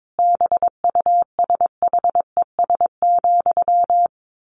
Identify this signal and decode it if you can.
Morse code, heard as a keyed tone, about 22 wpm, 700 Hz. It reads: BUSHES,
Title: A Sound Of Thunder 22wpm